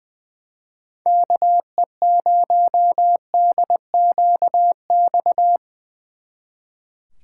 Morse code, heard as a keyed tone, about 22 wpm, 700 Hz. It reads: KE0DQX
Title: Speed Racing - USA 2x3 Call Signs 20wpm